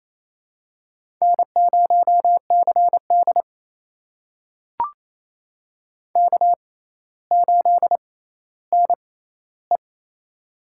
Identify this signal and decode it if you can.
Morse code, heard as a keyed tone, about 28 wpm, spaced as for 14 wpm, 700 Hz. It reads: N0CD K8NE
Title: USA Call Signs - 4 Characters - 6x Character Spacing - 28wpm